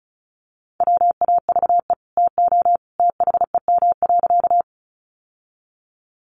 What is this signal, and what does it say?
Morse code, heard as a keyed tone, about 34 wpm, 700 Hz. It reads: WAVE TO THEM.
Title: Sentences from Top 1400 Words 35wpm